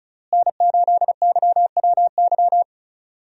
Morse code, heard as a keyed tone, about 34 wpm, 700 Hz. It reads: N8YWY
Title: USA Call Signs - 5 Characters - Repeated 3x - 35wpm